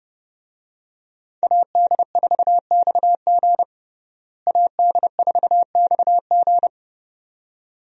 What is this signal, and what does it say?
Morse code, heard as a keyed tone, about 30 wpm, 700 Hz. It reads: AD4XG AD4XG
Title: USA Call Signs - 5 Characters - Repeated 2x - 30wpm